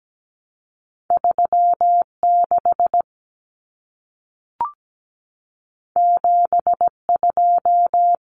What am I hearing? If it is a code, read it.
Morse code, heard as a keyed tone, about 17 wpm, 700 Hz. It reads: 3 6 7 2
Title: Common Cut Numbers - 2-Digit 17wpm